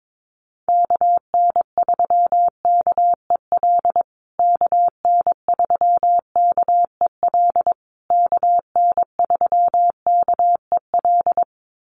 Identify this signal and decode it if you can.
Morse code, heard as a keyed tone, about 22 wpm, 700 Hz. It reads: KN3KEL KN3KEL KN3KEL
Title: USA Call Signs Repeated 3x 22wpm